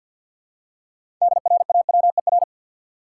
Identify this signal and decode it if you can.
Morse code, heard as a keyed tone, about 53 wpm, 700 Hz. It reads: DRAWER
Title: Six Letter Words - 50wpm